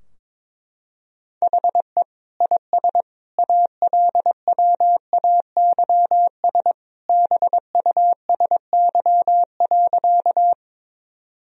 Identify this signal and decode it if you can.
Morse code, heard as a keyed tone, about 22 wpm, 700 Hz. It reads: HE IS ALWAYS BUSY.
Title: Sentences from Top 1000 Words - Review Part 3 - 22wpm